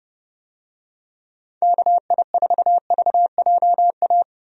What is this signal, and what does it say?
Morse code, heard as a keyed tone, about 32 wpm, 700 Hz. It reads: KI4VJA
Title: USA Call Signs - 6 Characters - Repeated 2x - 30wpm